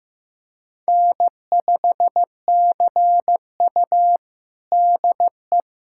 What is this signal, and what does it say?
Morse code, heard as a keyed tone, about 15 wpm, 700 Hz. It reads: N5CU DE
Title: USA Callsign DE Callsign - 4 Characters - 15wpm